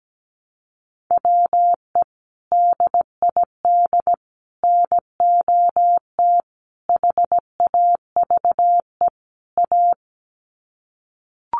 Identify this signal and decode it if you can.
Morse code, heard as a keyed tone, about 17 wpm, 700 Hz. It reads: WE DID NOT HAVE A
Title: Sets of 5 Words Encore2 17wpm